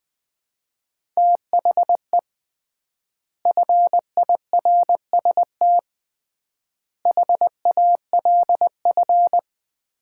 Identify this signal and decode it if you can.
Morse code, heard as a keyed tone, about 20 wpm, 700 Hz. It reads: THE FIRST HALF
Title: 3x Word-Spacing - Sets of 5 Words - 20wpm